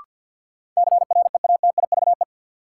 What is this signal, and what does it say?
Morse code, heard as a keyed tone, about 53 wpm, 700 Hz. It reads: CREATIVE
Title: Sentences from Top 1300 Words - New Word Drill Alt - 50wpm